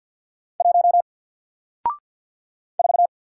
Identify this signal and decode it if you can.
Morse code, heard as a keyed tone, about 53 wpm, 700 Hz. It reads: 1 4
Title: Single Letter-Number Alt 50wpm